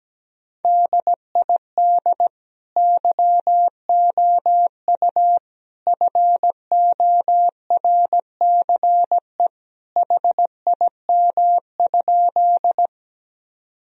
Morse code, heard as a keyed tone, about 17 wpm, 700 Hz. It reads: DID YOU FORCE HIM?
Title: Sentences from Top 1000 Words - Review Part 3 - 17wpm